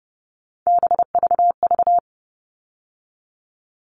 Morse code, heard as a keyed tone, about 30 wpm, 700 Hz. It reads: BVV
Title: All Cut Numbers - 3-Digit 30wpm